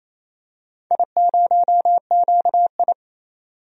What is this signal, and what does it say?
Morse code, heard as a keyed tone, about 28 wpm, 700 Hz. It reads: I0QI
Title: DXCC-100 Call Signs - 4 Characters - 28wpm